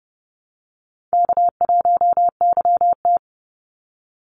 Morse code, heard as a keyed tone, about 30 wpm, 700 Hz. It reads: K1YT
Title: Easy Call Signs - 30wpm